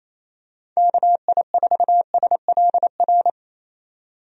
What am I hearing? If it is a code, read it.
Morse code, heard as a keyed tone, about 27 wpm, 700 Hz. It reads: KI4SLR
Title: USA Call Signs - 6 Characters - 28wpm